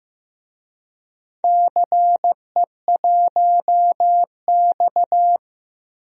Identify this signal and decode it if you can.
Morse code, heard as a keyed tone, about 16 wpm, 700 Hz. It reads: CE1X
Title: DXCC-100 Call Signs - 4 Characters - Repeated 2x - 15wpm